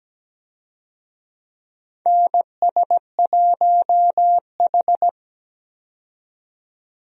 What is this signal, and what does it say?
Morse code, heard as a keyed tone, about 17 wpm, 700 Hz. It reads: NS1H
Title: USA Call Signs - 4 Characters - 17wpm